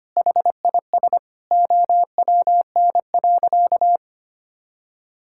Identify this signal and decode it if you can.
Morse code, heard as a keyed tone, about 25 wpm, 700 Hz. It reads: HIS OWN.
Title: Sentences from Top 1000 Words - Review Part 6 - 25wpm